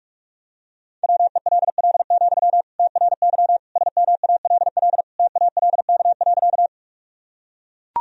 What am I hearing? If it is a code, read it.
Morse code, heard as a keyed tone, about 41 wpm, 700 Hz. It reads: WELL, TRY SMALL TALK.
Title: Sentences from Top 1000 Words - Review Part 1 - 45wpm